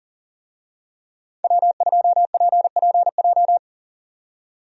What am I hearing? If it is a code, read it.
Morse code, heard as a keyed tone, about 39 wpm, 700 Hz. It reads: W2PPJ
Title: USA Call Signs - 5 Characters - 40wpm